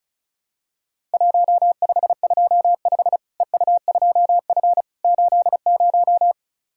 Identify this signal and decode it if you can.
Morse code, heard as a keyed tone, about 36 wpm, 700 Hz. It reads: 1525 EU2F 80
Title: Worked All Europe DX Contest - QTC Exchange - 35wpm